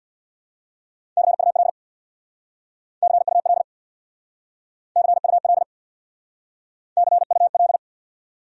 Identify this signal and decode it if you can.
Morse code, heard as a keyed tone, about 66 wpm, 700 Hz. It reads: CUL CUL CUL CUL